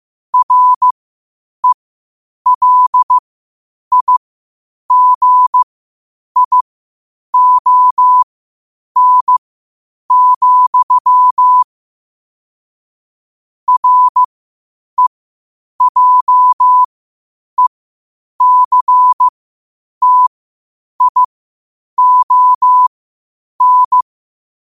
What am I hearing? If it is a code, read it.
Morse code, heard as a keyed tone, about 15 wpm, spaced as for 8 wpm, 1000 Hz. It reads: RELIGION, REJECTION